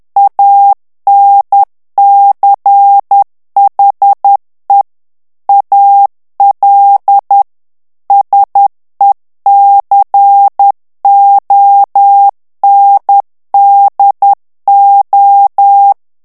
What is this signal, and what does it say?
Morse code, heard as a keyed tone, about 10 wpm, 800 Hz. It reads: ANCHE AL SECONDO
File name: ITALIEN texte clair 2 vitesse 10 mots minute